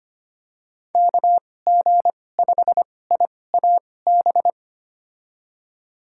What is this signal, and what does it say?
Morse code, heard as a keyed tone, about 25 wpm, spaced as for 18 wpm, 700 Hz. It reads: KG5IAB
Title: USA Call Signs - 6 Characters - 2x Character Spacing - 25wpm